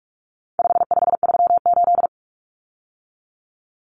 Morse code, heard as a keyed tone, about 44 wpm, 700 Hz. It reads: 5538
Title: Common Cut Numbers - 4-Digit 45wpm